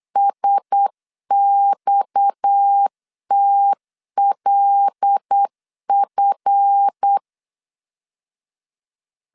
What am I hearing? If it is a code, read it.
Morse code, heard as a keyed tone, about 8 wpm, 800 Hz. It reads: SXTLF
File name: LEZIONI / 7 GF9Z / 7 GF9Z 8wpm 3213